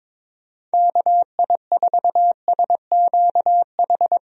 Morse code, heard as a keyed tone, about 22 wpm, 700 Hz. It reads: KI4SQH
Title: USA Call Signs - 6 Characters - Repeated 3x - 22wpm